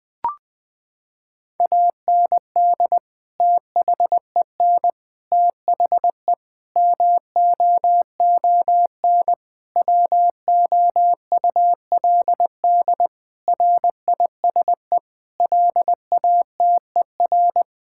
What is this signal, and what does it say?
Morse code, heard as a keyed tone, about 21 wpm, 700 Hz. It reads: AND THEN THE MOON WOULD RISE LATER
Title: Sentences from Top 1000 Words - Review Part 6 - 20wpm